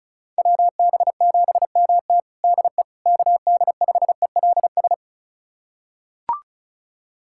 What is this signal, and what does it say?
Morse code, heard as a keyed tone, about 36 wpm, 690 Hz. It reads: WB7MT DE KD5ELS
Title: USA Callsign DE Callsign 35wpm